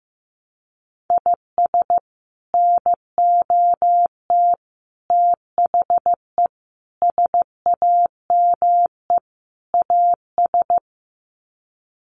Morse code, heard as a keyed tone, about 15 wpm, 700 Hz. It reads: IS NOT THE SAME AS
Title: Sets of 5 Words Encore 15wpm